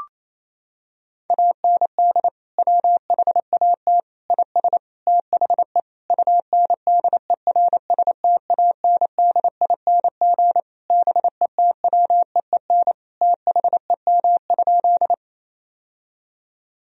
Morse code, heard as a keyed tone, about 28 wpm, 700 Hz. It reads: AND WHAT IS THE UNDERSTANDING BETWEEN THEM?
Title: Sentences from Top 1000 Words - Review Part 3 - 28wpm